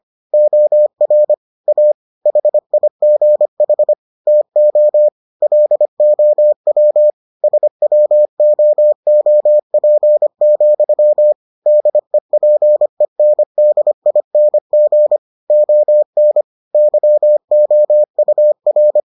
Morse code, heard as a keyed tone, about 25 wpm, 600 Hz. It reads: OR A HIGH TO LOW SWOOP, DEPENDING ON YOUR